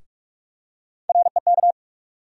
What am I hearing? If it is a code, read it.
Morse code, heard as a keyed tone, about 46 wpm, 700 Hz. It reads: REK
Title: USA Call Sign Postfixes - 3 Letters 45wpm